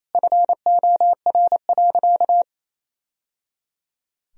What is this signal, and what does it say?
Morse code, heard as a keyed tone, about 28 wpm, 700 Hz. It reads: FOR.
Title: Sentences from Top 1000 Words - Review Part 2 - 28wpm